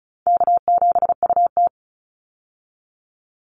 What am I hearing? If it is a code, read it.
Morse code, heard as a keyed tone, about 34 wpm, 700 Hz. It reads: K7UT
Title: Easy Call Signs Repeated 2x - 35wpm